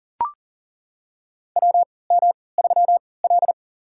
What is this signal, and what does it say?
Morse code, heard as a keyed tone, about 40 wpm, spaced as for 23 wpm, 700 Hz. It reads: WM3L
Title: USA Call Signs - 4 Characters - 3x Character Spacing - 40wpm